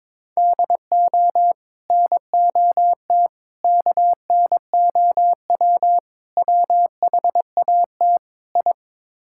Morse code, heard as a keyed tone, about 22 wpm, 700 Hz. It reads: DO NOT KNOW WHAT I
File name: practice / 22wpm / Build A Phrase - 5-Word Phrases - 22wpm